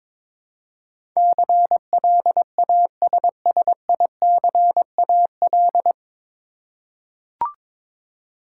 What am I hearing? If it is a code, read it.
Morse code, heard as a keyed tone, about 22 wpm, 700 Hz. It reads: CLASSICAL
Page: Morse Code Ninja practice 22wpm